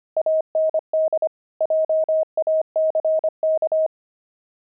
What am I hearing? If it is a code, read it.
Morse code, heard as a keyed tone, about 25 wpm, 630 Hz. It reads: AND JACK
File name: pangram-fs-25-cs-25-grumpy